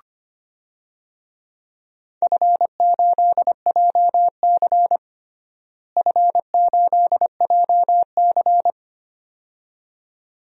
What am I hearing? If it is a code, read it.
Morse code, heard as a keyed tone, about 25 wpm, 700 Hz. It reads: F8JC F8JC
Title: DXCC-100 Call Signs - 4 Characters - Repeated 2x - 25wpm